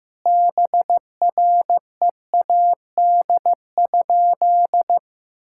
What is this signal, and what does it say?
Morse code, heard as a keyed tone, about 16 wpm, 700 Hz. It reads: BREAD?
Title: Sentences from Top 1600 Words 15wpm